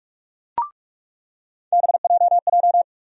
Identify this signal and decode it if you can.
Morse code, heard as a keyed tone, about 45 wpm, 700 Hz. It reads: BJJ
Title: USA Call Sign Postfixes - 3 Letters 45wpm